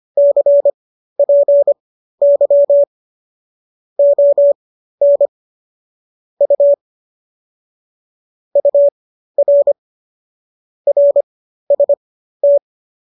Morse code, heard as a keyed tone, about 25 wpm, spaced as for 13 wpm, 570 Hz. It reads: CPY ON U UR RST